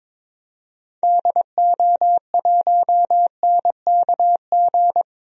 Morse code, heard as a keyed tone, about 22 wpm, 700 Hz. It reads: DO1NKG
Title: DXCC-100 Call Signs - 6 Characters - 22wpm